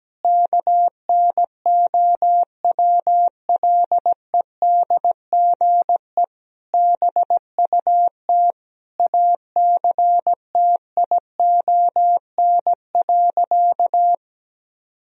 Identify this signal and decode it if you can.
Morse code, heard as a keyed tone, about 17 wpm, 700 Hz. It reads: KNOWLEDGE BUT ACTION.
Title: Sentences from Top 1000 Words - Review Part 4 - 17wpm